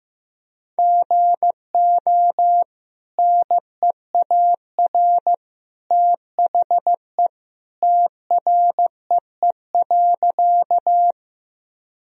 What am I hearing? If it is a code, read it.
Morse code, heard as a keyed tone, about 15 wpm, 700 Hz. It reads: GO NEAR THE TREE.
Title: Sentences from Top 1000 Words - Review Part 1 - 15wpm